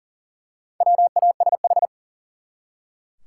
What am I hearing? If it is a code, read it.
Morse code, heard as a keyed tone, about 40 wpm, 700 Hz. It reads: WASH
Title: Four Letter Words - 40wpm